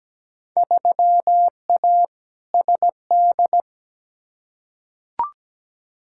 Morse code, heard as a keyed tone, about 17 wpm, 700 Hz. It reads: 3A SD
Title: Field Day Exchanges - 17wpm